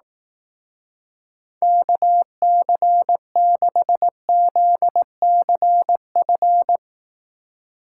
Morse code, heard as a keyed tone, about 18 wpm, 700 Hz. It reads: KC6ZCF
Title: Speed Racing - USA 2x3 Call Signs 15wpm